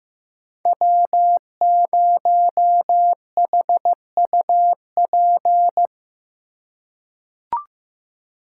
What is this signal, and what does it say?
Morse code, heard as a keyed tone, about 15 wpm, 700 Hz. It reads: W0HUP
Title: USA Call Signs - 2x Character Spacing - 15wpm